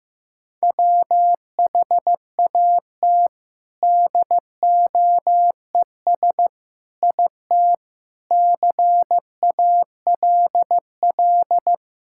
Morse code, heard as a keyed tone, about 16 wpm, 700 Hz. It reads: WHAT DOES IT CALL
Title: Sentences from Top 1000 Words - Review Part 5 - 15wpm